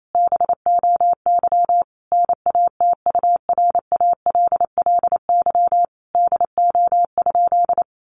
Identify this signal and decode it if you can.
Morse code, heard as a keyed tone, about 28 wpm, 700 Hz. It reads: BOY NATURALLY DO?
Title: Sentences from Top 1300 Words 28wpm